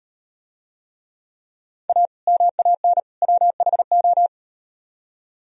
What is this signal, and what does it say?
Morse code, heard as a keyed tone, about 38 wpm, 700 Hz. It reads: A MAN WHO